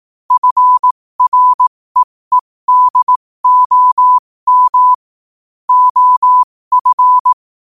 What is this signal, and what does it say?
Morse code, heard as a keyed tone, about 18 wpm, spaced as for 15 wpm, 1000 Hz. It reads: FREEDOM OF
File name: qotdfiles / 15wpm